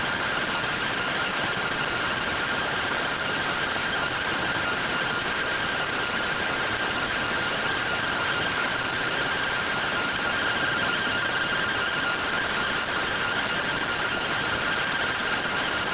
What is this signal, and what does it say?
Two-tone teleprinter signal, about 170 Hz shift.